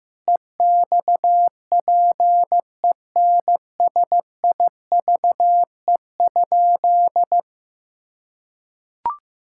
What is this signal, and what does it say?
Morse code, heard as a keyed tone, about 16 wpm, 700 Hz. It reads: EXPENSIVE?
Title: Sentences from Top 1000 Words - Review Part 6 - 15wpm